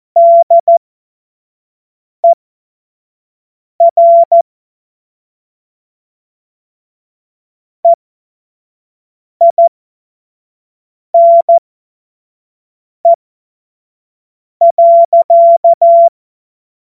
Morse code, heard as a keyed tone, about 14 wpm, spaced as for 5 wpm, 680 Hz. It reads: DER EINE.
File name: Morsekurs / Witze / Witz 08. Februar 2022 25 F70